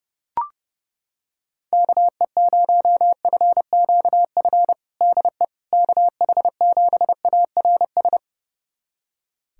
Morse code, heard as a keyed tone, about 29 wpm, 700 Hz. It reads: KE0FQF DE KH7ARS
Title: USA Callsign DE Callsign - 6 Characters - 30wpm